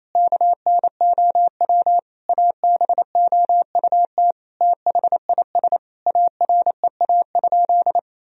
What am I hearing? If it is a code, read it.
Morse code, heard as a keyed tone, about 28 wpm, 700 Hz. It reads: KNOW ABOUT THIS AREA?
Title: Sentences from Top 1000 Words - Review Part 2 - 28wpm